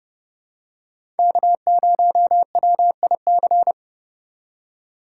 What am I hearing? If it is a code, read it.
Morse code, heard as a keyed tone, about 30 wpm, 700 Hz. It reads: K0WIC
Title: USA Call Signs - 5 Characters - 2x Character Spacing - 30wpm